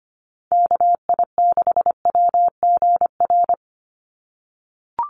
Morse code, heard as a keyed tone, about 25 wpm, 700 Hz. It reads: KI6WGR
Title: USA Call Signs 25wpm